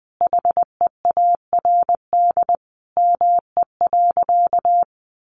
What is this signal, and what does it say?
Morse code, heard as a keyed tone, about 20 wpm, 700 Hz. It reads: HEARD ME.
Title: Sentences from Top 400 Words 20wpm